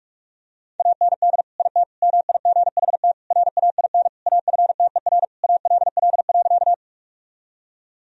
Morse code, heard as a keyed tone, about 46 wpm, 700 Hz. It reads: AND IT MIGHT RAIN AFTER ALL.
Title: Sentences from Top 1000 Words - Review Part 1 - 45wpm